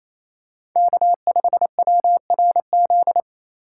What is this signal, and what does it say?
Morse code, heard as a keyed tone, about 28 wpm, 700 Hz. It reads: K5WRZ
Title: USA Call Signs - 5 Characters - 2x Character Spacing - 28wpm